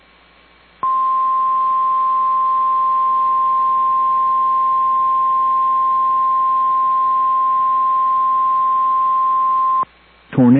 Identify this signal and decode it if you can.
Emergency Alert System (SAME) tones (the 1050 Hz attention tone), with speech. It is heard from 0.8 to 9.9 s.
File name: EAS Logs / KMNR